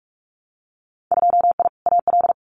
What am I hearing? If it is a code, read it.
Morse code, heard as a keyed tone, about 46 wpm, 700 Hz. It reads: 2I AL
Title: Winter Field Day Exchanges 45wpm